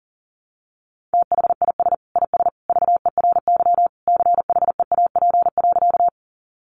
Morse code, heard as a keyed tone, about 40 wpm, 700 Hz. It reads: THIS IS VERY CHEAP.
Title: Sentences from Top 1200 Words 40wpm